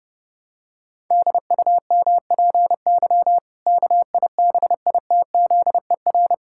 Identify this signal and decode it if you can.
Morse code, heard as a keyed tone, about 32 wpm, 700 Hz. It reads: DUMPY KIBITZER
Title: Pangram Challenge v2 - 30wpm